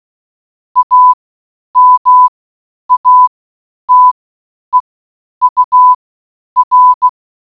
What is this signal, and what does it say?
Morse code, heard as a keyed tone, about 16 wpm, spaced as for 9 wpm, 1000 Hz. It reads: AMATEUR